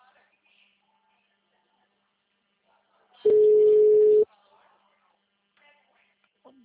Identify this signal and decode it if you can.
Ringback tone (Europe, 425 Hz), with speech.
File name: out- -2002- - - .1094